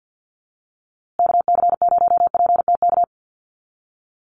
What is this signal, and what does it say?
Morse code, heard as a keyed tone, about 50 wpm, 700 Hz. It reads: KC0PTK
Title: USA Call Signs Repeated 2x 50wpm